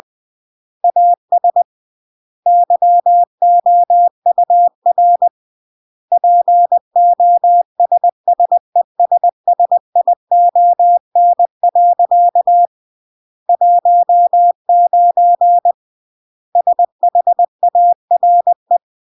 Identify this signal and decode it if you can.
Morse code, heard as a keyed tone, about 20 wpm, 700 Hz. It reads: AS YOUR POSSESSION. 19 SHARE